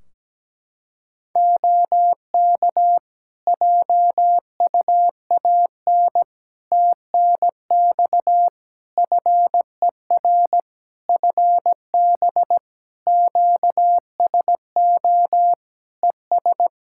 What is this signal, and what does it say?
Morse code, heard as a keyed tone, about 17 wpm, 700 Hz. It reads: OK JUAN TNX FER FB QSO ES
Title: Practice QSO - Exchange 4 - Ending - 17wpm